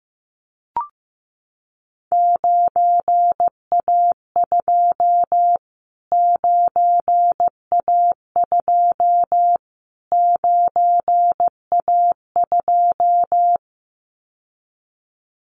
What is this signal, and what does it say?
Morse code, heard as a keyed tone, about 15 wpm, 700 Hz. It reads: 9A2 9A2 9A2
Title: Callsign Prefixes for Top 25 DXCC - Repeated 3x - 15wpm